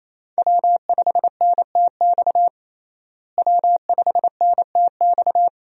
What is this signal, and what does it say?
Morse code, heard as a keyed tone, about 28 wpm, 700 Hz. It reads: W5NTX W5NTX
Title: USA Call Signs - 5 Characters - Repeated 2x - 28wpm